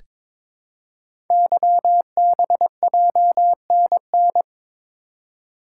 Morse code, heard as a keyed tone, about 22 wpm, 700 Hz. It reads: YBJNN
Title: Code Groups - 5 Letters - 22wpm